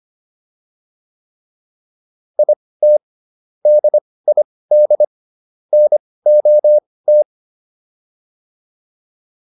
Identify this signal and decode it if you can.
Morse code, heard as a keyed tone, about 25 wpm, spaced as for 18 wpm, 600 Hz. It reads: IT DID NOT